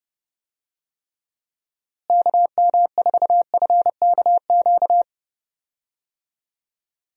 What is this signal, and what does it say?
Morse code, heard as a keyed tone, about 31 wpm, 700 Hz. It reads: KM4FKQ
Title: USA Call Signs - 6 Characters - Repeated 2x - 30wpm